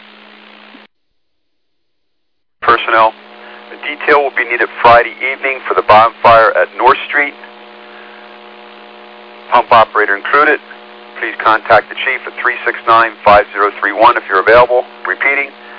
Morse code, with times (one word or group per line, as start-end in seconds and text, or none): none